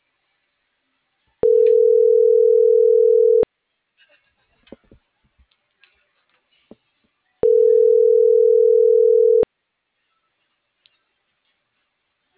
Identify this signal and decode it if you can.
Ringback tone (North America), 440 + 480 Hz.